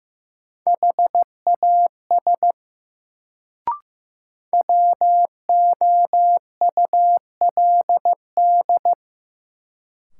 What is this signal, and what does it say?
Morse code, heard as a keyed tone, about 15 wpm, 700 Hz. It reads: HAS WOULD
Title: Sentences from Top 100 Words - New Word Drill - 15wpm